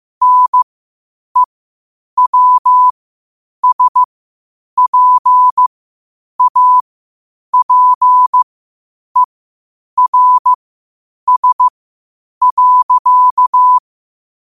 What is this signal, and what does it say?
Morse code, heard as a keyed tone, about 15 wpm, spaced as for 8 wpm, 1000 Hz. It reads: NEWSPAPERS.
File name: qotdfiles / 8wpm